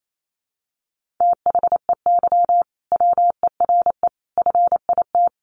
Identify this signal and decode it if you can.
Morse code, heard as a keyed tone, about 28 wpm, 700 Hz. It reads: THEY WERE FIT
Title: Sentences from Top 800 Words 28wpm